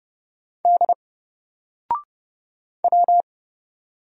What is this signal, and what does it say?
Morse code, heard as a keyed tone, about 31 wpm, 700 Hz. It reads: D W
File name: Single-Letter-Number-30wpm